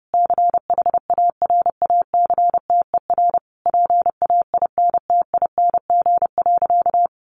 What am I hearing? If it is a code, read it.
Morse code, heard as a keyed tone, about 30 wpm, 700 Hz. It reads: CHARACTER PAINTING.
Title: Sentences from Top 1200 Words 30wpm